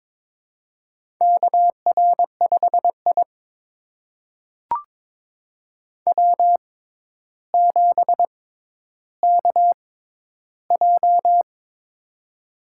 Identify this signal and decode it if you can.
Morse code, heard as a keyed tone, about 22 wpm, spaced as for 11 wpm, 700 Hz. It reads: KR5I W7KJ
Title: USA Call Signs - 4 Characters - 6x Character Spacing - 22wpm